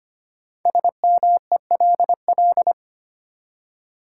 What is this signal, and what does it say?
Morse code, heard as a keyed tone, about 26 wpm, 700 Hz. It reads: SMELL
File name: 5 Letter Words - First 3 Letters Given - 25wpm